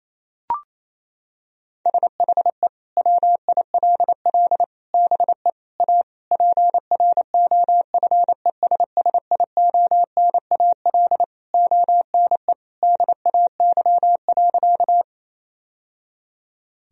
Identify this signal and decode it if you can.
Morse code, heard as a keyed tone, about 27 wpm, 700 Hz. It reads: SHE WILL BE A PROFESSIONAL ONE DAY.
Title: Sentences from Top 1000 Words - Review Part 4 - 28wpm